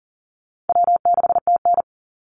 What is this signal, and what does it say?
Morse code, heard as a keyed tone, about 40 wpm, 700 Hz. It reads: W6TN
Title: Easy Call Signs - 40wpm